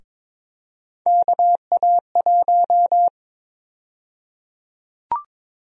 Morse code, heard as a keyed tone, about 22 wpm, 700 Hz. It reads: KA1
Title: USA Call Sign Prefixes - Repeated 2x - 22wpm